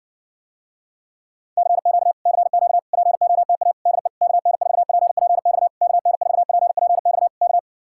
Morse code, heard as a keyed tone, about 60 wpm, 700 Hz. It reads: CQ CQ POTA DE KN4PPX KN4PPX K